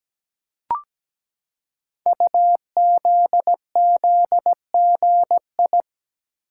Morse code, heard as a keyed tone, about 17 wpm, 700 Hz. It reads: UZZGI
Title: Code Groups - 5 Letters - 17wpm